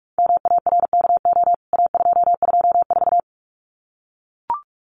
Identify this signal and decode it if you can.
Morse code, heard as a keyed tone, about 44 wpm, 700 Hz. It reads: MARKO A224
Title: CWT Exchange 45wpm